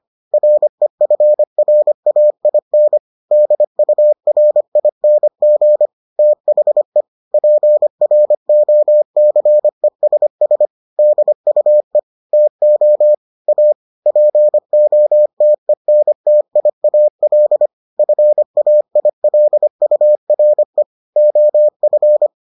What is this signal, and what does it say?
Morse code, heard as a keyed tone, about 25 wpm, 600 Hz. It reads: REFRAIN DURING THE PROCESS DUE TO A POTENTIAL FAILURE OF